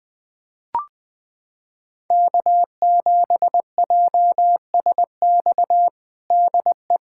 Morse code, heard as a keyed tone, about 21 wpm, 700 Hz. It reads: K7JSX DE
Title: USA Callsign DE Callsign - 5 Characters - 20wpm